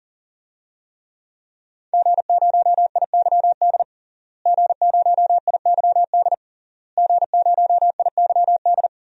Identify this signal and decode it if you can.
Morse code, heard as a keyed tone, about 40 wpm, 700 Hz. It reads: G0IYD G0IYD G0IYD
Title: DXCC-100 Call Signs - 5 Characters - Repeated 3x - 40wpm